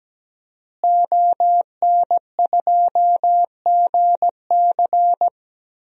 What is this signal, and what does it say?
Morse code, heard as a keyed tone, about 17 wpm, 700 Hz. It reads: ON2GC
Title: DXCC-100 Call Signs - 5 Characters - Repeated 3x - 17wpm